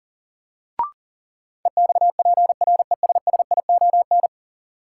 Morse code, heard as a keyed tone, about 40 wpm, 700 Hz. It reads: EXPRESSION
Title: Sentences from Top 1200 Words - New Word Drill Alt - 40wpm